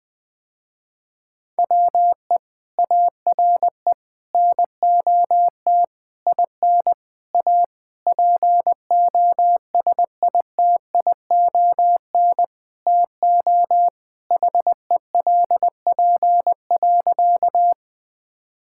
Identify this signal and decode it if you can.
Morse code, heard as a keyed tone, about 20 wpm, 700 Hz. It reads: WE ARE NOT IN A POSITION TO HELP.
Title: Sentences with 7-Word Phrases - Spoken Beginning - 20wpm